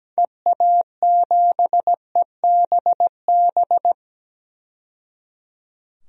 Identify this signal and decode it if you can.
Morse code, heard as a keyed tone, about 17 wpm, 700 Hz. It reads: EA7EBB
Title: DXCC-100 Call Signs - 6 Characters - 17wpm